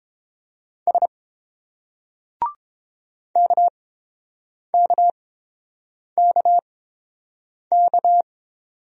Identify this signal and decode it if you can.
Morse code, heard as a keyed tone, about 29 wpm, spaced as for 7 wpm, 700 Hz. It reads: S KKKK